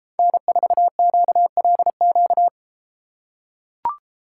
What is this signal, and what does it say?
Morse code, heard as a keyed tone, about 32 wpm, 700 Hz. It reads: N4QLQ